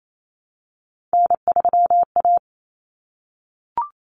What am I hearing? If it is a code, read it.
Morse code, heard as a keyed tone, about 28 wpm, 700 Hz. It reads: N3A
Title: Common Cut Numbers - 3-Digit 28wpm